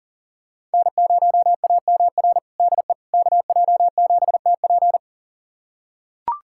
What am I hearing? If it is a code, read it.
Morse code, heard as a keyed tone, about 40 wpm, 690 Hz. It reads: N0AMR DE KJ7TP
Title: USA Callsign DE Callsign - 5 Characters - 40wpm